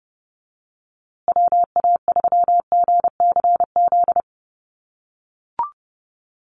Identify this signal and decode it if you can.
Morse code, heard as a keyed tone, about 30 wpm, 700 Hz. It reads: WA3GCZ
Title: USA Call Signs Repeated 3x 30wpm